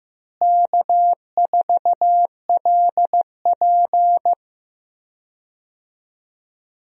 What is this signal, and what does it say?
Morse code, heard as a keyed tone, about 15 wpm, 700 Hz. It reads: K4LP
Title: USA Call Signs - 4 Characters - Repeated 2x - 15wpm